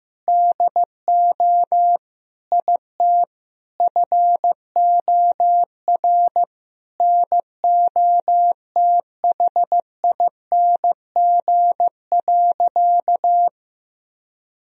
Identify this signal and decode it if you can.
Morse code, heard as a keyed tone, about 15 wpm, 700 Hz. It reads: DO IT FOR NOTHING.
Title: Sentences from Top 1000 Words - Review Part 3 - 15wpm